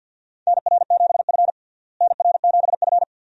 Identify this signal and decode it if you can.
Morse code, heard as a keyed tone, about 51 wpm, 700 Hz. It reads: NR7F NR7F
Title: USA Call Signs - 4 Characters - Repeated 2x - 50wpm